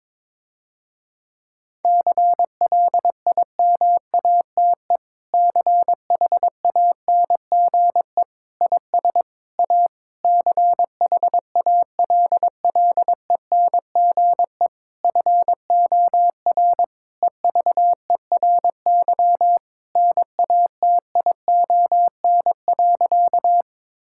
Morse code, heard as a keyed tone, about 22 wpm, 700 Hz. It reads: CLIMATE CHANGE IS A CHALLENGE FOR EVERY NATION.
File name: Difficult Sentences with Hints - 22wpm